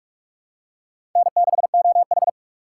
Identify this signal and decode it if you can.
Morse code, heard as a keyed tone, about 45 wpm, 700 Hz. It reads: N6OH
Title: USA Call Signs - 4 Characters - 4x Character Spacing - 45wpm